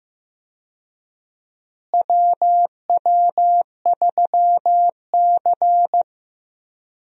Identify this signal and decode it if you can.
Morse code, heard as a keyed tone, about 15 wpm, 700 Hz. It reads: WW3C
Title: USA Call Signs - 4 Characters - Repeated 3x - 15wpm